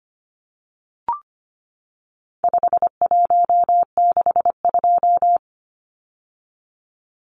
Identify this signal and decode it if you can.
Morse code, heard as a keyed tone, about 25 wpm, 700 Hz. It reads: 5162
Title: Numbers - 4-Digit 25wpm